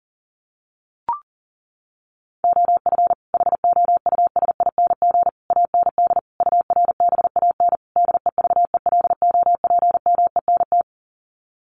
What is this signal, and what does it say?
Morse code, heard as a keyed tone, about 40 wpm, 700 Hz. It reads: OF HOUSING AND URBAN DEVELOPMENT
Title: Sets of 5 Words Encore2 40wpm